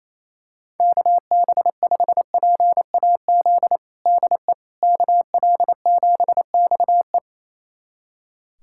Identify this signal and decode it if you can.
Morse code, heard as a keyed tone, about 28 wpm, 700 Hz. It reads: KB5PAZ DE KL7XE
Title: USA Callsign DE Callsign 28wpm